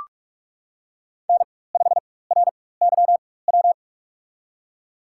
Morse code, heard as a keyed tone, about 44 wpm, spaced as for 21 wpm, 700 Hz. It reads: N5RYW
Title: USA Call Signs - 5 Characters - 4x Character Spacing - 45wpm